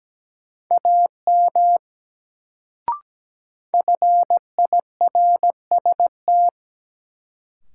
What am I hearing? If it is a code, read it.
Morse code, heard as a keyed tone, about 17 wpm, 700 Hz. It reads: AM FIRST
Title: Sets of 2 Words - New Word Drill - 17wpm